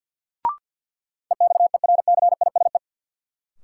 Morse code, heard as a keyed tone, about 52 wpm, 700 Hz. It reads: EXERCISE
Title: Sentences from Top 1200 Words - New Word Drill Alt - 50wpm